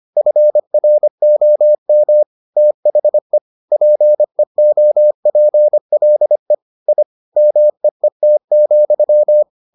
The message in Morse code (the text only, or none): FROM THE PEOPLE I MEET,